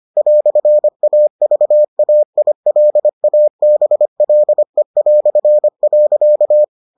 Morse code, heard as a keyed tone, about 25 wpm, 600 Hz. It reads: "AVAILABLE".